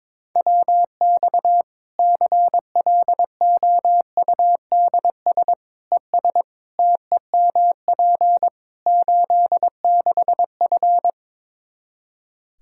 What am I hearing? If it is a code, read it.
Morse code, heard as a keyed tone, about 22 wpm, 700 Hz. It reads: WX CLOUDS ES TEMP 86F
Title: Practice QSO - Exchange 2 - Info - 22wpm